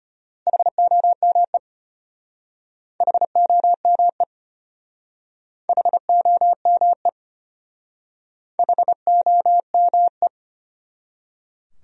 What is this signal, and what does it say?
Morse code, heard as a keyed tone, about 31 wpm, 700 Hz. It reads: HOME HOME HOME HOME
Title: Speed-Racing Top 200 Words 25wpm